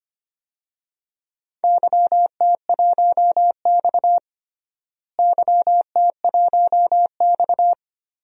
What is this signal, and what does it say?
Morse code, heard as a keyed tone, about 25 wpm, 700 Hz. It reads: YT1X YT1X
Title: DXCC-100 Call Signs - 4 Characters - Repeated 3x - 25wpm